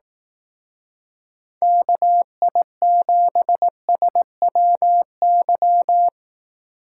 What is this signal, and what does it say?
Morse code, heard as a keyed tone, about 18 wpm, 700 Hz. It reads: KI7SWY
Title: Speed Racing - USA 2x3 Call Signs 15wpm